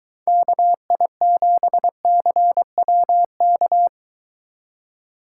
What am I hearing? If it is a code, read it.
Morse code, heard as a keyed tone, about 23 wpm, 700 Hz. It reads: KI7CWK